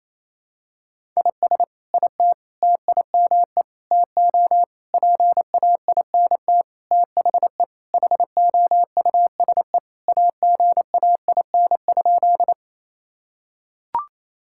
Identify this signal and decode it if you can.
Morse code, heard as a keyed tone, about 27 wpm, 700 Hz. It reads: IS IT TIME TO PAINT THE HOUSE AGAIN?
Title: Sentences from Top 1000 Words - Review Part 6 - 28wpm